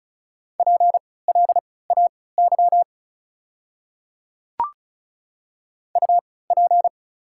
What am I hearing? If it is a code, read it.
Morse code, heard as a keyed tone, about 36 wpm, spaced as for 20 wpm, 700 Hz. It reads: PLAY UP
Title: 3x Character Spacing - Top 200 Words - 35wpm